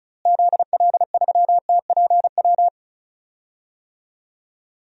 Morse code, heard as a keyed tone, about 36 wpm, 700 Hz. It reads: ZL3TPW
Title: DXCC-100 Call Signs - 6 Characters - Repeated 3x - 35wpm